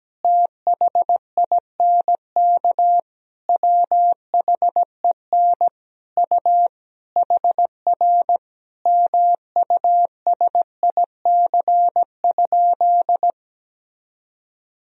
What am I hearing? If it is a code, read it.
Morse code, heard as a keyed tone, about 17 wpm, 700 Hz. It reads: THINK WHEN U HR MUSIC?
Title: Sentences from Top 300 Words and Abbreviations 17wpm